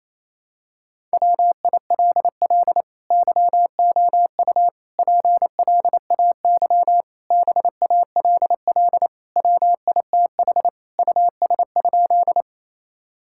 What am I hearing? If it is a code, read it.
Morse code, heard as a keyed tone, about 27 wpm, 700 Hz. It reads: WILL YOU PLAY BALL WITH US?
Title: Sentences from Top 1000 Words - Review Part 6 - 28wpm